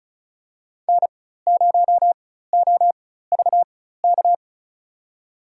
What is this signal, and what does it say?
Morse code, heard as a keyed tone, about 34 wpm, spaced as for 16 wpm, 700 Hz. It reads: N0OVK
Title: USA Call Signs - 4x Character Spacing - 35wpm